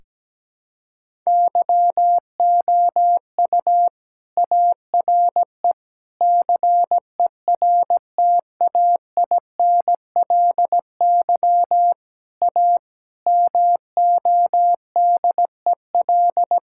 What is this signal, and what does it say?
Morse code, heard as a keyed tone, about 17 wpm, 700 Hz. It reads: YOU ARE CERTAINLY A MODEL
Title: Sentences from Top 1000 Words - Review Part 5 - 17wpm